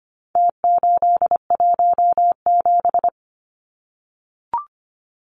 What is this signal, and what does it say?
Morse code, heard as a keyed tone, about 25 wpm, 700 Hz. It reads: T817
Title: Numbers - Contest Style - 4-Digit 25wpm